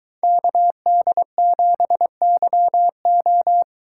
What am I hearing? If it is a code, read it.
Morse code, heard as a keyed tone, about 22 wpm, 700 Hz. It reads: KD7YO